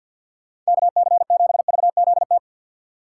Morse code, heard as a keyed tone, about 65 wpm, 700 Hz. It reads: KC7VZT